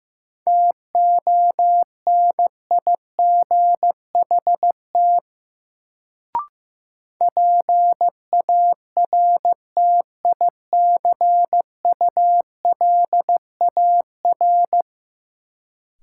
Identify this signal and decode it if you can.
Morse code, heard as a keyed tone, about 15 wpm, 700 Hz. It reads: TONIGHT PARTICULAR
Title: Sentences from Top 600 Words - New Word Drill - 15wpm